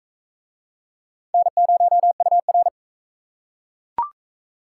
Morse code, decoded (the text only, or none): N0UR